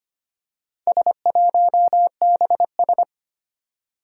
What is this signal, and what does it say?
Morse code, heard as a keyed tone, about 25 wpm, 700 Hz. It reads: S1BS
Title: Four Challenging Letter-Number Combinations 25wpm